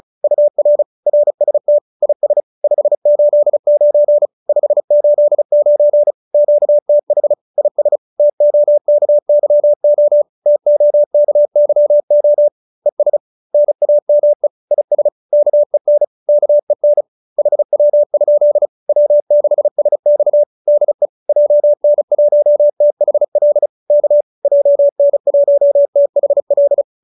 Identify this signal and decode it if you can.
Morse code, heard as a keyed tone, about 34 wpm, 590 Hz. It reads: UR RST IS 589 589 QTH IS TOKYO TOKYO ES NAME IS KEN KEN HW? W6SX DE JN1THL K JN1THL